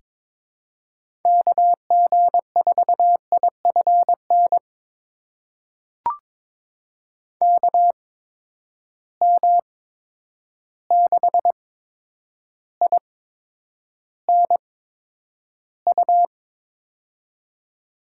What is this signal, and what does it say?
Morse code, heard as a keyed tone, about 22 wpm, spaced as for 9 wpm, 700 Hz. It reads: KG4IFN KM6INU
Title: USA Call Signs - 6 Characters - 8x Character Spacing - 22wpm